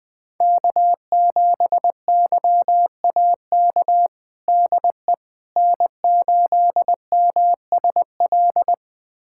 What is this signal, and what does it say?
Morse code, heard as a keyed tone, about 21 wpm, 700 Hz. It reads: K7YAK DE N8MSL
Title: USA Callsign DE Callsign - 5 Characters - 20wpm